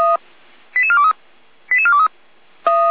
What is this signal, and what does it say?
Five-tone selective call (ZVEI), heard as quick runs of five tones, tones of about 80 ms. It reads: 88423 88423